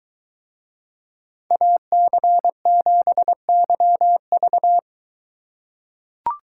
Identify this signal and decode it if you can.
Morse code, heard as a keyed tone, about 23 wpm, 700 Hz. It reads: AC7YV